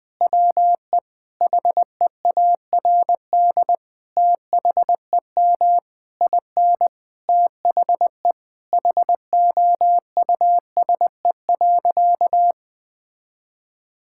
Morse code, heard as a keyed tone, about 21 wpm, 700 Hz. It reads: WE HEARD THEM IN THE HOUSE.
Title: Sentences from Top 1000 Words - Review Part 5 - 20wpm